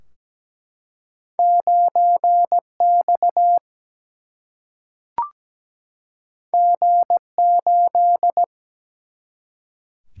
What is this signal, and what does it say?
Morse code, heard as a keyed tone, about 17 wpm, 700 Hz. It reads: 9X G8
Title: Single-Letter Single-Number 17wpm